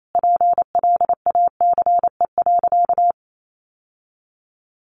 Morse code, heard as a keyed tone, about 28 wpm, 700 Hz. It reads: PLACE.
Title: Sentences from Top 1400 Words 28wpm